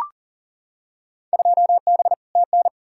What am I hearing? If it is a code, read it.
Morse code, heard as a keyed tone, about 40 wpm, 700 Hz. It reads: 2B TN
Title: Field Day Exchanges Encore - 40wpm